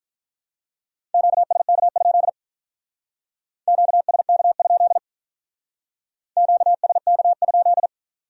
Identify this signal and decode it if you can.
Morse code, heard as a keyed tone, about 44 wpm, 700 Hz. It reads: QSK? QSK? QSK?